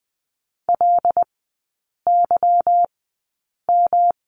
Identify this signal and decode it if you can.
Morse code, heard as a keyed tone, about 20 wpm, spaced as for 8 wpm, 690 Hz. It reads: LYM
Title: 145 Lesson 24e - Letter y - Three Characters 20wpm